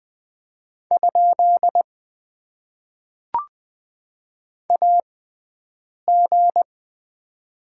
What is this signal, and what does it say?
Morse code, heard as a keyed tone, about 20 wpm, spaced as for 7 wpm, 700 Hz. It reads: ? AG
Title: Top 90 QSO Elements - 6x Character Spacing - 20wpm